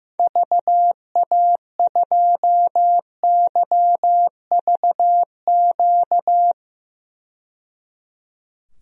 Morse code, heard as a keyed tone, about 16 wpm, 700 Hz. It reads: VA2YVQ
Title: DXCC-100 Call Signs - 6 Characters - 15wpm